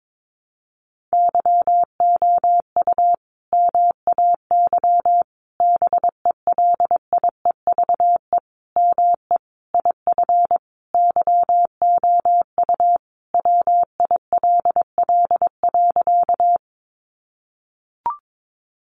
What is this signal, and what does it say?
Morse code, heard as a keyed tone, about 22 wpm, 700 Hz. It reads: YOU MAY BELIEVE ME IF YOU WILL.
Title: Sentences from Top 1000 Words 22wpm